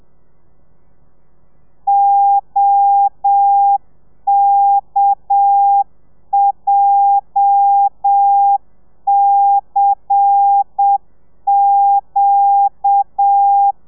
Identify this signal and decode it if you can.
Morse code, heard as a keyed tone, about 7 wpm, 800 Hz. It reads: OKJCQ
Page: Lezione 1: audio a 30 CPM